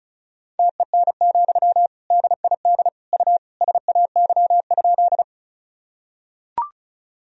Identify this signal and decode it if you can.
Morse code, heard as a keyed tone, about 36 wpm, 700 Hz. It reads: TEN, DID U SAY?
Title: Sentences from Top 400 Words and Abbreviations 35wpm